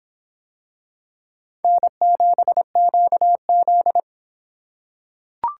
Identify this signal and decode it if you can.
Morse code, heard as a keyed tone, about 22 wpm, 700 Hz. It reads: N7QZ